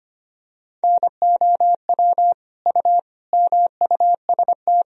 Morse code, heard as a keyed tone, about 25 wpm, 700 Hz. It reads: NOW U MUST
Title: Sentences from Top 300 Words and Abbreviations 25wpm